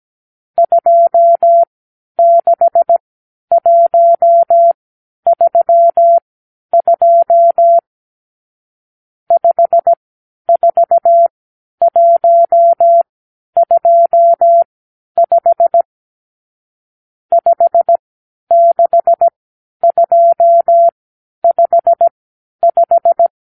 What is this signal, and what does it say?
Morse code, heard as a keyed tone, about 17 wpm, spaced as for 10 wpm, 680 Hz. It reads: 26132 54125 56255
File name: k7qo / Lesson 50 (1-6)